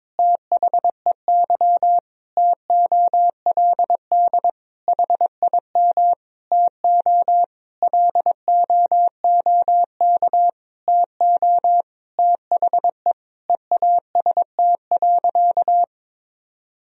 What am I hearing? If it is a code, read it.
Morse code, heard as a keyed tone, about 22 wpm, 700 Hz. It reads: THEY TOLD HIM TO LOOK TO THE EAST.
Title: Sentences from Top 1000 Words - Review Part 3 - 22wpm